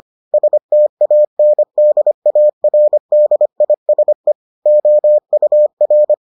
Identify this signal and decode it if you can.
Morse code, heard as a keyed tone, about 25 wpm, 600 Hz. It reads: STANDARDISE OUR